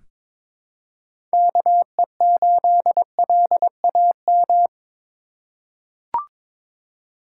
Morse code, heard as a keyed tone, about 21 wpm, 700 Hz. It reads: KE8LAM
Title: USA Call Signs - 6 Characters - 3x Character Spacing - 22wpm